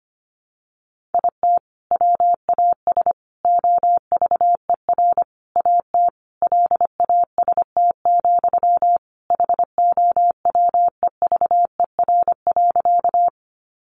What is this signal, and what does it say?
Morse code, heard as a keyed tone, about 25 wpm, 700 Hz. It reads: IT WAS OVER AT LAST, HOWEVER.
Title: Sentences from Top 600 Words 25wpm